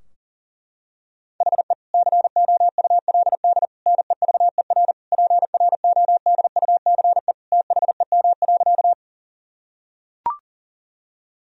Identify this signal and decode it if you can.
Morse code, heard as a keyed tone, about 37 wpm, 700 Hz. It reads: HE COULD NEVER PRODUCE THEM.
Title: Sentences from Top 1000 Words - Review Part 3 - 40wpm